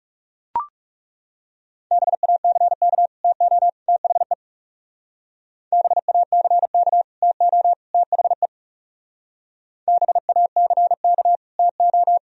Morse code, heard as a keyed tone, about 37 wpm, 700 Hz. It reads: BACK TO THE BACK TO THE BACK TO
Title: Speed-Racing - Sets of 3 Words 30wpm